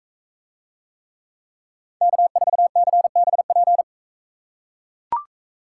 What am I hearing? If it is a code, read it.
Morse code, heard as a keyed tone, about 42 wpm, 700 Hz. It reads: K4CBP